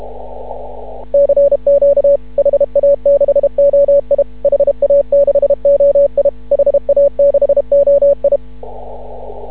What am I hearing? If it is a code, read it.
Morse code, heard as a keyed tone, about 31 wpm, 560 Hz. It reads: CQ HA6OI HA6OI HA6OI
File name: CW / ha6oi 2